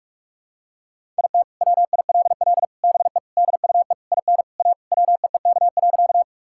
Morse code, heard as a keyed tone, about 42 wpm, 700 Hz. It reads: IT WILL BE DUE IN A WEEK.
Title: Sentences from Top 1000 Words - Review Part 6 - 45wpm